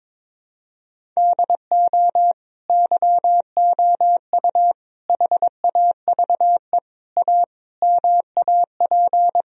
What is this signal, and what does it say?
Morse code, heard as a keyed tone, about 22 wpm, 700 Hz. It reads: DO YOU HAVE A MAP
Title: Sentences from Top 1000 Words - Review Part 2 - 22wpm